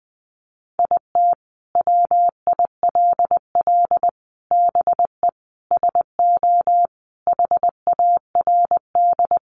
Morse code, heard as a keyed tone, about 20 wpm, 700 Hz. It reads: IT WILL BE SO HARD
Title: Sentences from Top 1500 Words 20wpm